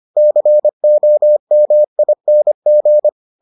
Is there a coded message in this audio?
Yes: Morse code from 0.2 to 3.1 s, heard as a keyed tone, about 25 wpm, 600 Hz. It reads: COMING